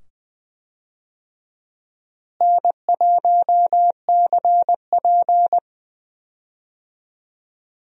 Morse code, heard as a keyed tone, about 20 wpm, 700 Hz. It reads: N1CP
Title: USA Call Signs - 4 Characters - Repeated 2x - 20wpm